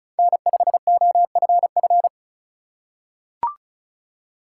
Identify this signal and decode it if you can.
Morse code, heard as a keyed tone, about 36 wpm, 700 Hz. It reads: N5OFF
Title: USA Call Signs - 5 Characters - 4x Character Spacing - 35wpm